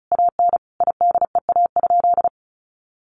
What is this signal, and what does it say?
Morse code, heard as a keyed tone, about 34 wpm, 700 Hz. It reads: AN IDEA?
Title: Sentences from Top 400 Words 35wpm